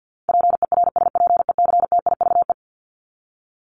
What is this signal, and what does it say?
Morse code, heard as a keyed tone, about 50 wpm, 700 Hz. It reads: PERSPECTIVE
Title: Top 1001-1250 Words 50wpm